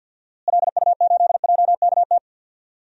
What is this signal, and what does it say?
Morse code, heard as a keyed tone, about 52 wpm, 700 Hz. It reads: LU8JKT
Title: DXCC-100 Call Signs - 6 Characters - Repeated 3x - 50wpm